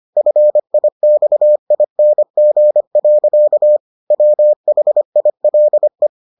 Morse code, heard as a keyed tone, about 25 wpm, 600 Hz. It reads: FIXING. WHILE